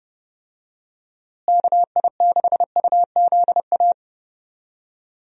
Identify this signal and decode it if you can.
Morse code, heard as a keyed tone, about 32 wpm, 700 Hz. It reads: KI6UZA
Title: USA Call Signs - 6 Characters - Repeated 2x - 30wpm